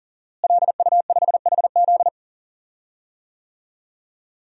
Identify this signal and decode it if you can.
Morse code, heard as a keyed tone, about 40 wpm, 700 Hz. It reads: LU5HZ
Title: DXCC-100 Call Signs - 5 Characters - Repeated 2x - 40wpm